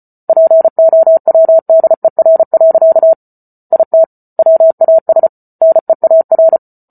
Morse code, heard as a keyed tone, about 34 wpm, 650 Hz. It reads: POWDER. IT WAS NEAR